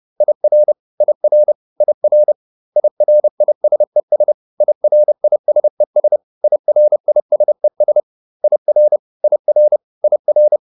Morse code, heard as a keyed tone, about 30 wpm, 600 Hz. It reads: IR IR IR IRISES IRISES IRISES IR IR IR